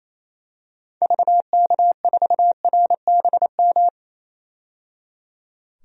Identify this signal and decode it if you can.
Morse code, heard as a keyed tone, about 28 wpm, 700 Hz. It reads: VK4RBM
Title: DXCC-100 Call Signs - 6 Characters - 28wpm